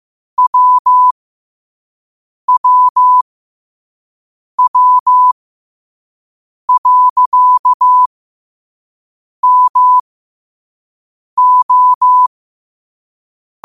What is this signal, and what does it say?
Morse code, heard as a keyed tone, about 15 wpm, spaced as for 5 wpm, 1000 Hz. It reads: WWW.MO